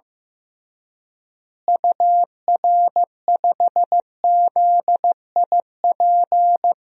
Morse code, heard as a keyed tone, about 15 wpm, 700 Hz. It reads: UR5ZIP
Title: DXCC-100 Call Signs - 6 Characters - Repeated 3x - 15wpm